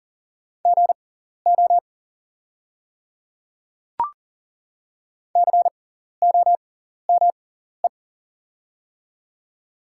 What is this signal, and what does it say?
Morse code, heard as a keyed tone, about 40 wpm, spaced as for 14 wpm, 700 Hz. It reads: GO COME
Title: 6x Character Spacing - Top 100 Words - 40wpm